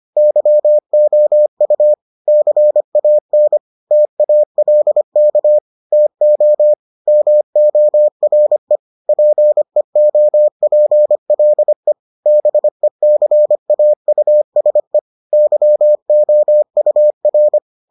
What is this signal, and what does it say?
Morse code, heard as a keyed tone, about 25 wpm, 600 Hz. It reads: YOU CAN TALK TO MORE PEOPLE BECAUSE YOUR